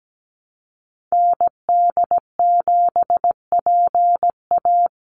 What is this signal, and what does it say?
Morse code, heard as a keyed tone, about 17 wpm, 700 Hz. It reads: ND7PA
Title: NA SOTA Activator Callsigns 17wpm